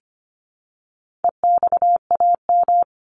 Morse code, heard as a keyed tone, about 25 wpm, 700 Hz. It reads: EXAM
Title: Top 501-750 Words 25wpm